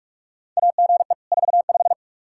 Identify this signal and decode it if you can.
Morse code, heard as a keyed tone, about 46 wpm, 700 Hz. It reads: AGE 45
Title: Practice QSO - Exchange 3 - More Info - 45wpm